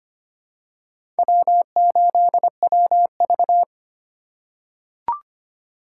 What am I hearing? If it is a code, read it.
Morse code, heard as a keyed tone, about 25 wpm, 700 Hz. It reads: W8WV
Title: USA Call Signs - 4 Characters - 3x Character Spacing - 25wpm